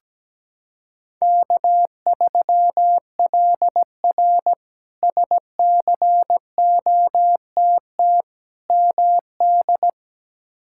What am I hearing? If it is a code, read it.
Morse code, heard as a keyed tone, about 17 wpm, 700 Hz. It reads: K3LR SCOTT MD
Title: K1USN SST QSO Practice - 17wpm